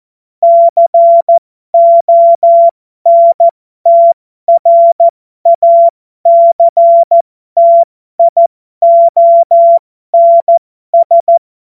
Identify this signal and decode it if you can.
Morse code, heard as a keyed tone, about 14 wpm, spaced as for 12 wpm, 680 Hz. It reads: CONTRACTIONS
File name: Morsekurs / Witze